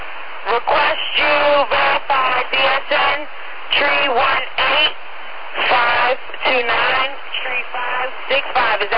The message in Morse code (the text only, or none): none